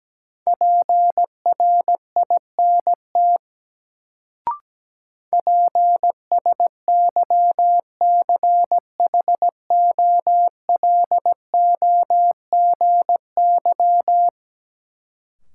Morse code, heard as a keyed tone, about 17 wpm, 700 Hz. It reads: PRINT PSYCHOLOGY
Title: Sentences from Top 1400 Words - New Word Drill Alt - 17wpm